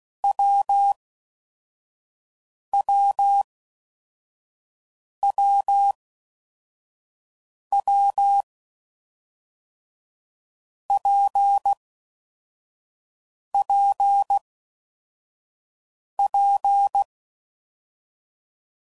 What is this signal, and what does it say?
Morse code, heard as a keyed tone, about 16 wpm, 800 Hz. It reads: W W W W P P P
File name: Combinations / Difficult Combos / WPJ WPJ1